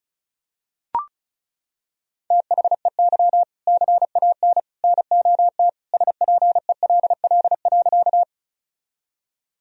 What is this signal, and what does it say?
Morse code, heard as a keyed tone, about 36 wpm, 700 Hz. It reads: THEY CAN NOT SPELL.
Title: Sentences from Top 1000 Words - Review Part 1 - 35wpm